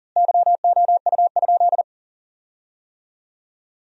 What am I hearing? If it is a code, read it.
Morse code, heard as a keyed tone, about 40 wpm, 700 Hz. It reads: YOU?
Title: 4x Word-Spacing - Sentences from Top 100 Words - 40wpm